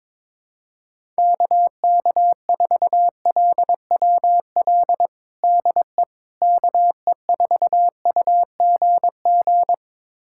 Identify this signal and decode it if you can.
Morse code, heard as a keyed tone, about 22 wpm, 700 Hz. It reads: KK4LWL DE KE4UGG
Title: USA Callsign DE Callsign - 6 Characters - 22wpm